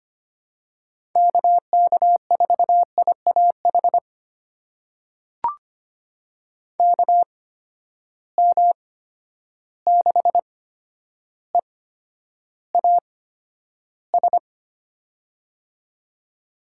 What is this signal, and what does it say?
Morse code, heard as a keyed tone, about 25 wpm, spaced as for 11 wpm, 700 Hz. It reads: KK4IAH KM6EAS
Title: USA Call Signs - 6 Characters - 8x Character Spacing - 25wpm